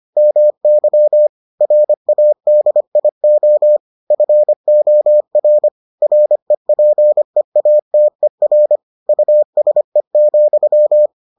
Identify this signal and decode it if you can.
Morse code, heard as a keyed tone, about 25 wpm, 600 Hz. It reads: MY RADIO FOR REPEATER USE,